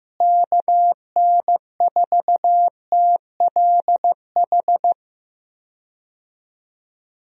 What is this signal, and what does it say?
Morse code, heard as a keyed tone, about 15 wpm, 700 Hz. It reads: KN4TLH
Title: USA Call Signs - 6 Characters - Repeated 3x - 15wpm